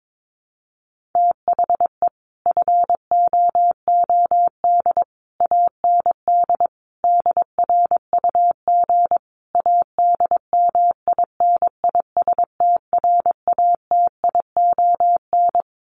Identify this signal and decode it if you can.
Morse code, heard as a keyed tone, about 22 wpm, 700 Hz. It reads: THE FOOD AND DRUG ADMINISTRATION
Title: Sets of 5 Words 22wpm